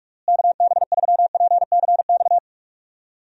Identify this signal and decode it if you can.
Morse code, heard as a keyed tone, about 46 wpm, 700 Hz. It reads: KB3PCX
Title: USA Call Signs - 6 Characters - Repeated 3x - 45wpm